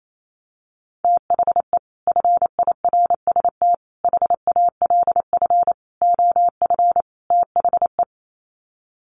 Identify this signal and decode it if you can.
Morse code, heard as a keyed tone, about 28 wpm, 700 Hz. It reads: THE FIRST HALF OF THE
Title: Sets of 5 Words 28wpm